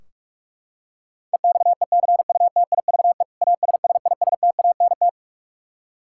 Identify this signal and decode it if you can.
Morse code, heard as a keyed tone, about 41 wpm, 700 Hz. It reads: EXECUTIVE ASSISTANT
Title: Practice QSO - Exchange 3 - More Info - 45wpm